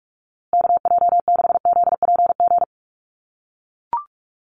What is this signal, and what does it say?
Morse code, heard as a keyed tone, about 44 wpm, 700 Hz. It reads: KJ6ZPG
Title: USA Call Signs Repeated 3x 45wpm